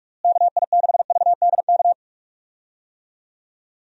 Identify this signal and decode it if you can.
Morse code, heard as a keyed tone, about 42 wpm, 700 Hz. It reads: KI6VDK
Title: USA Call Signs - 6 Characters - Repeated 2x - 45wpm